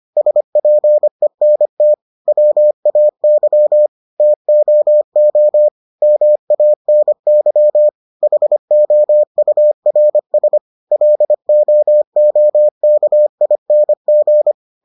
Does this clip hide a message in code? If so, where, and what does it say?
Yes: Morse code from 0.2 to 14.5 s, heard as a keyed tone, about 25 wpm, 600 Hz. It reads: SPENT WAY TOO MANY HOURS LOOKING